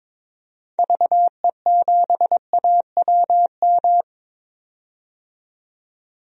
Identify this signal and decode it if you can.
Morse code, heard as a keyed tone, about 22 wpm, 700 Hz. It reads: VE7AWM
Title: DXCC-100 Call Signs - 6 Characters - 22wpm